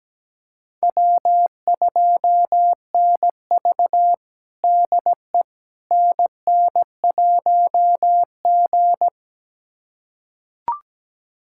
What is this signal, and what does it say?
Morse code, heard as a keyed tone, about 17 wpm, 700 Hz. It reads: W2NV DE NN1G
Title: USA Callsign DE Callsign - 4 Characters - 17wpm